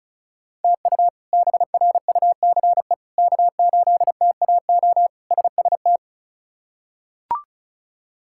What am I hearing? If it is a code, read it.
Morse code, heard as a keyed tone, about 34 wpm, 700 Hz. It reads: TU BRUCE K8TAO SST
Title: K1USN SST QSO Practice - 35wpm